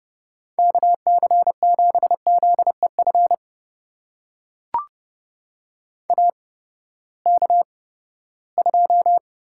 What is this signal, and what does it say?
Morse code, heard as a keyed tone, about 32 wpm, spaced as for 17 wpm, 700 Hz. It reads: KC7ZEF AK2
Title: USA Call Signs - 8x Character Spacing - 30wpm